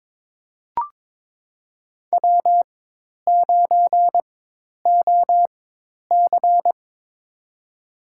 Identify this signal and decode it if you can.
Morse code, heard as a keyed tone, about 22 wpm, spaced as for 10 wpm, 700 Hz. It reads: W9OC
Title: USA Call Signs - 4x Character Spacing - 22wpm